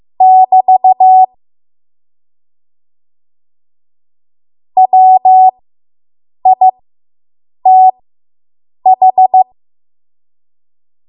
Morse code, heard as a keyed tone, about 15 wpm, spaced as for 7 wpm, 750 Hz. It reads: <BT> WITH